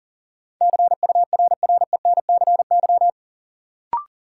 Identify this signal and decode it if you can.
Morse code, heard as a keyed tone, about 40 wpm, 700 Hz. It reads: CURRENCY
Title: Sentences from Top 1500 Words - New Word Drill Alt - 40wpm